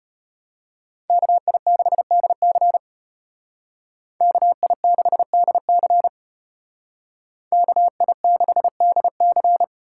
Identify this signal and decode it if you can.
Morse code, heard as a keyed tone, about 30 wpm, 700 Hz. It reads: KI6DC KI6DC KI6DC
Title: Speed Racing - USA 2x2 Call Signs 25wpm